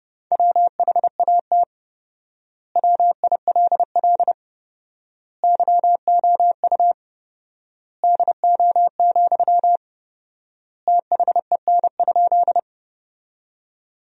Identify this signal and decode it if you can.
Morse code, heard as a keyed tone, about 30 wpm, 700 Hz. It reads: WHAT WILL YOU DO, THEN?
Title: 4x Word-Spacing - Sentences from Top 100 Words - 30wpm